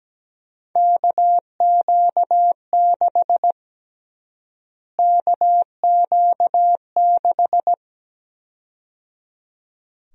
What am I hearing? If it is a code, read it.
Morse code, heard as a keyed tone, about 17 wpm, 700 Hz. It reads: KQ6 KQ6
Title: USA Call Sign Prefixes - Repeated 2x - 17wpm